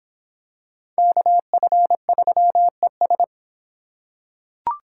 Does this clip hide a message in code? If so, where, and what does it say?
Yes: Morse code from 1.0 to 3.2 s, heard as a keyed tone, about 24 wpm, 700 Hz. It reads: KF3ES